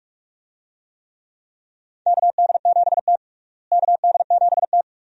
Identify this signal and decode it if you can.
Morse code, heard as a keyed tone, about 46 wpm, 700 Hz. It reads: KD7T KD7T
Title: USA Call Signs - 4 Characters - Repeated 2x - 45wpm